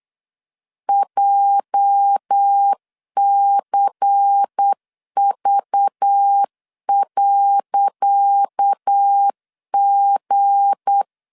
Morse code, heard as a keyed tone, about 8 wpm, 800 Hz. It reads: JCV.G
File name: LEZIONI / 10 SIGNS / 10C DOT 8wpm 3649